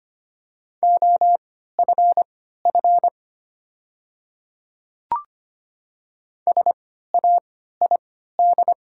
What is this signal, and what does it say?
Morse code, heard as a keyed tone, about 25 wpm, spaced as for 14 wpm, 700 Hz. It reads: OFF SAID
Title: 3x Character Spacing - Top 300 Words - 25wpm